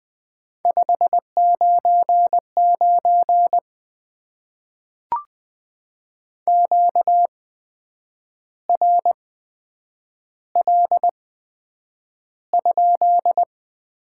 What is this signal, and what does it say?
Morse code, heard as a keyed tone, about 20 wpm, spaced as for 8 wpm, 700 Hz. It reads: 599 QRL?
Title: Top 90 QSO Elements - 8x Character Spacing - 20wpm